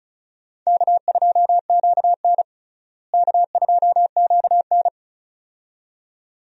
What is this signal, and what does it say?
Morse code, heard as a keyed tone, about 36 wpm, 700 Hz. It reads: K2QN K2QN
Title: USA Call Signs - 4 Characters - Repeated 2x - 35wpm